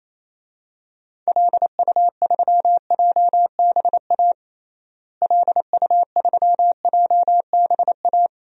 Morse code, heard as a keyed tone, about 27 wpm, 700 Hz. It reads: LU3JBA LU3JBA
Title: DXCC-100 Call Signs - 6 Characters - Repeated 2x - 28wpm